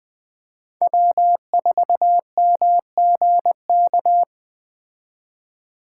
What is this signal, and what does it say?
Morse code, heard as a keyed tone, about 21 wpm, 700 Hz. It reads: W4MGK
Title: Speed Racing - USA 1x3 Call Signs 17wpm